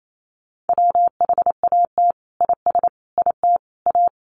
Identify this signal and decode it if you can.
Morse code, heard as a keyed tone, about 28 wpm, 700 Hz. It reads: WHAT IS IT A
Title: Sentences from Top 1300 Words 28wpm